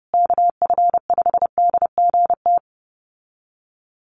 Morse code, heard as a keyed tone, about 30 wpm, 700 Hz. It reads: KF5DGT
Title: USA Call Signs Repeated 3x 30wpm